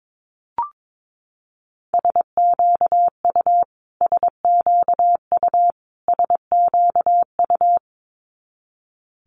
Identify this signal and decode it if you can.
Morse code, heard as a keyed tone, about 22 wpm, 700 Hz. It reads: SQU SQU SQU
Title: Multi-letter Phonemes Repeated 3x 22wpm